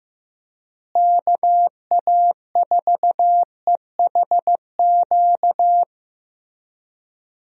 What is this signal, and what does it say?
Morse code, heard as a keyed tone, about 15 wpm, 700 Hz. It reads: KA4EHQ
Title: USA Call Signs - 6 Characters - Repeated 3x - 15wpm